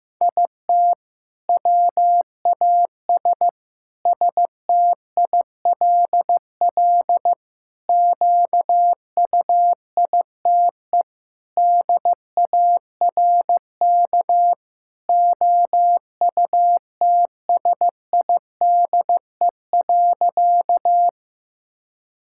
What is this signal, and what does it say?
Morse code, heard as a keyed tone, about 15 wpm, 700 Hz. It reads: IT WAS STILL QUITE DARK OUTSIDE.
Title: Sentences from Top 1000 Words - Review Part 6 - 15wpm